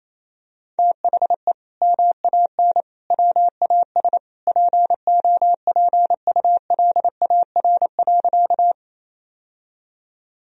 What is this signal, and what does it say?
Morse code, heard as a keyed tone, about 28 wpm, 700 Hz. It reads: THE MAN WAS POPULAR.
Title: Sentences from Top 1000 Words - Review Part 3 - 28wpm